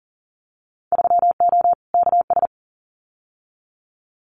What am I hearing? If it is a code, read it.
Morse code, heard as a keyed tone, about 41 wpm, 700 Hz. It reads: 3O KS
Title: Winter Field Day Exchanges Encore 40wpm